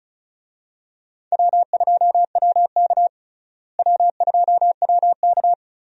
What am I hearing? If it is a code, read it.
Morse code, heard as a keyed tone, about 36 wpm, 700 Hz. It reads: W2WK W2WK
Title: USA Call Signs - 4 Characters - Repeated 3x - 35wpm